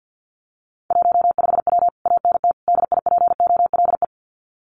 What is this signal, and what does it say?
Morse code, heard as a keyed tone, about 50 wpm, 700 Hz. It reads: 15W ANT DIPOLE
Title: Common QSO Phrases Encore 50wpm